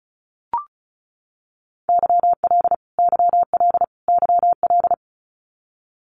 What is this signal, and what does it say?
Morse code, heard as a keyed tone, about 34 wpm, 700 Hz. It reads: YL YL YL
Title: Abbreviations Repeated 3x - 35wpm